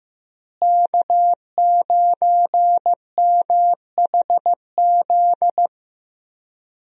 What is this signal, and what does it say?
Morse code, heard as a keyed tone, about 15 wpm, 700 Hz. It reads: K9MHZ
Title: USA Call Signs - 5 Characters - Repeated 2x - 15wpm